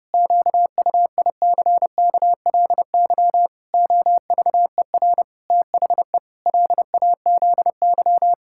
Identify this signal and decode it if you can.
Morse code, heard as a keyed tone, about 30 wpm, 700 Hz. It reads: QUICKLY OVER THE LAZY
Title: Pangram Challenge v2 - 30wpm